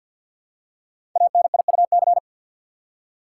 Morse code, heard as a keyed tone, about 45 wpm, 700 Hz. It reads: ANIUC
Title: Code Groups - 5 Letter:Numbers - 50wpm